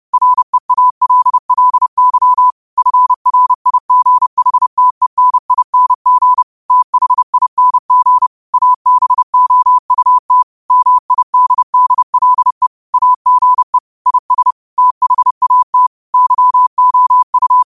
Morse code, heard as a keyed tone, about 30 wpm, 1000 Hz. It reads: REALLY FRIGHTENING THING ABOUT MIDDLE AGE IS THAT YOU
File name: qotdfiles / 30wpm